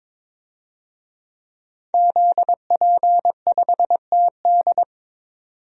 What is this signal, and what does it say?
Morse code, heard as a keyed tone, about 21 wpm, 700 Hz. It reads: ZP5TD
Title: DXCC-100 Call Signs - 5 Characters - 22wpm